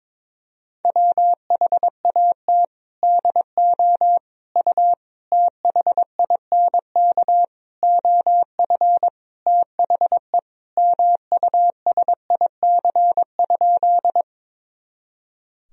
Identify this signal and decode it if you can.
Morse code, heard as a keyed tone, about 22 wpm, 700 Hz. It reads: WHAT DO U THINK OF THE MUSIC?
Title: Sentences from Top 300 Words and Abbreviations 22wpm